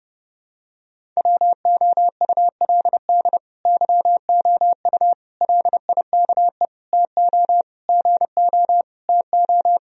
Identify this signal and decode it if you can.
Morse code, heard as a keyed tone, about 32 wpm, 700 Hz. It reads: WOULD YOU LIKE TO GO TO
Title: Sentences from Top 1000 Words - Review Part 1 - 30wpm